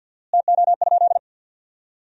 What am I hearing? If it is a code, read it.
Morse code, heard as a keyed tone, about 52 wpm, 700 Hz. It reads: TO?
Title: 4x Word-Spacing - Sentences from Top 500 Words - 50wpm